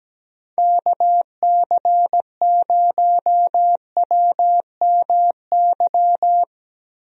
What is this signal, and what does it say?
Morse code, heard as a keyed tone, about 17 wpm, 700 Hz. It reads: KC0WMY
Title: USA Call Signs - 6 Characters - Repeated 2x - 17wpm